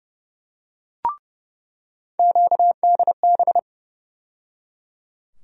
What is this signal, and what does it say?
Morse code, heard as a keyed tone, about 30 wpm, 700 Hz. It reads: QDB
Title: USA Call Sign Postfixes - 3 Letters 30wpm